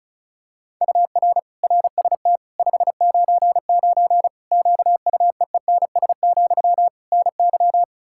Morse code, heard as a keyed tone, about 36 wpm, 700 Hz. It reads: UR RST 599 QUEENS, NY
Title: Practice QSO - Exchange 1 - Beginning - 35wpm